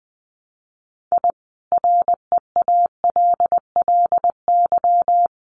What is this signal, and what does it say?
Morse code, heard as a keyed tone, about 20 wpm, 700 Hz. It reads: I REALLY
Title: Sets of 5 Words Encore2 20wpm